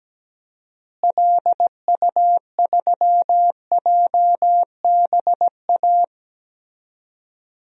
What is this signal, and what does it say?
Morse code, heard as a keyed tone, about 17 wpm, 700 Hz. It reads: LU3JBA
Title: DXCC-100 Call Signs - 6 Characters - Repeated 2x - 17wpm